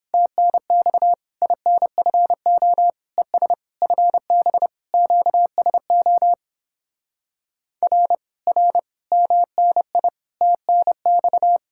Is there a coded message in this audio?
Yes: Morse code from 0.1 to 11.6 s, heard as a keyed tone, about 29 wpm, 700 Hz. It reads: TNX INFO ES FB QSO R R MNI TNX